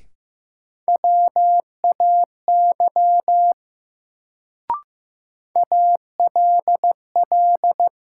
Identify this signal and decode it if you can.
Morse code, heard as a keyed tone, about 15 wpm, 700 Hz. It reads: WAY ALL
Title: Sentences from Top 100 Words - New Word Drill - 15wpm